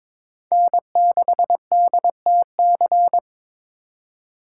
Morse code, heard as a keyed tone, about 22 wpm, 700 Hz. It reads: N6DTC
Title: USA Call Signs - 5 Characters - 2x Character Spacing - 22wpm